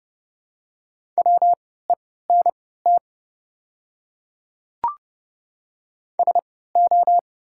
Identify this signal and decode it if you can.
Morse code, heard as a keyed tone, about 31 wpm, spaced as for 17 wpm, 690 Hz. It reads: WENT SO
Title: 3x Character Spacing - Top 300 Words - 30wpm